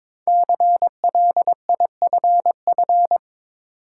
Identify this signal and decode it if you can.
Morse code, heard as a keyed tone, about 22 wpm, 700 Hz. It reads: CLIFF
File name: 5 Letter Words - First 3 Letters Given - 22wpm